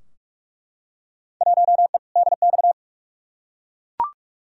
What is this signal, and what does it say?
Morse code, heard as a keyed tone, about 46 wpm, 700 Hz. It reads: 1E DX
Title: Field Day Exchanges Encore - 45wpm